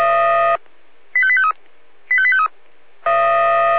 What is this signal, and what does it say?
Five-tone selective call (ZVEI), heard as quick runs of five tones, tones of about 70 ms. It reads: 85853 85853